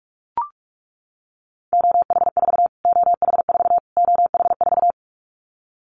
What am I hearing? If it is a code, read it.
Morse code, heard as a keyed tone, about 44 wpm, 700 Hz. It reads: OH4 OH4 OH4
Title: Callsign Prefixes for Top 25 DXCC - Repeated 3x - 45wpm